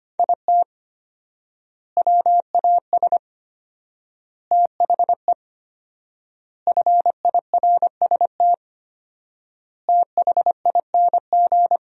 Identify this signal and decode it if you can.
Morse code, heard as a keyed tone, about 25 wpm, 700 Hz. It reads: IT WAS THE FIRST THING
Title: 4x Word-Spacing - Sentences from Top 300 Words - 25wpm